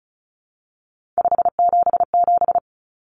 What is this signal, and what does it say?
Morse code, heard as a keyed tone, about 34 wpm, 700 Hz. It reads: 577
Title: Numbers - 3-Digit 35wpm